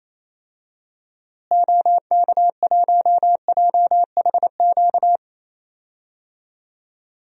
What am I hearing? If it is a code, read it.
Morse code, heard as a keyed tone, about 28 wpm, 700 Hz. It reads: OK1JHQ
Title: DXCC-100 Call Signs - 6 Characters - Repeated 2x - 28wpm